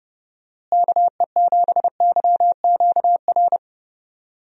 Morse code, heard as a keyed tone, about 30 wpm, 700 Hz. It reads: KE7YQR
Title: USA Call Signs - 6 Characters - Repeated 2x - 30wpm